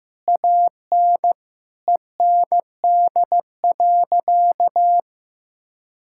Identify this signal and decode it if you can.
Morse code, heard as a keyed tone, about 15 wpm, 700 Hz. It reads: AN END.
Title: Sentences from Top 1000 Words - Review Part 6 - 15wpm